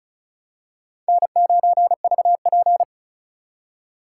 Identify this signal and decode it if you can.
Morse code, heard as a keyed tone, about 34 wpm, 700 Hz. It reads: N9VP
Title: USA Call Signs - 4 Characters - 2x Character Spacing - 35wpm